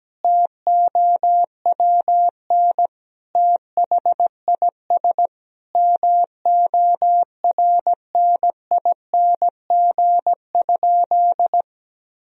Morse code, heard as a keyed tone, about 17 wpm, 700 Hz. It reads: TOWN THIS MORNING?
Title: Sentences from Top 1000 Words - Review Part 2 - 17wpm